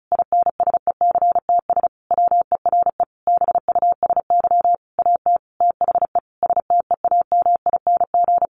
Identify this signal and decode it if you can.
Morse code, heard as a keyed tone, about 34 wpm, 700 Hz. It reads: INSECTS WERE BUSY AT THE STEAMING
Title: A Sound Of Thunder 35wpm